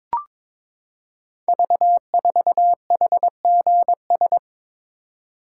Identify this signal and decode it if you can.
Morse code, heard as a keyed tone, about 23 wpm, 700 Hz. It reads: V4HGS
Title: Code Groups - 5 Letter:Numbers - 22wpm